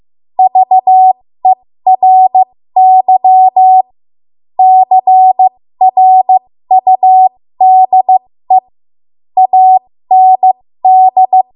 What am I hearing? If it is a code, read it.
Morse code, heard as a keyed tone, about 15 wpm, spaced as for 13 wpm, 750 Hz. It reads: VERY CRUDE AND
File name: Morse / 13WPM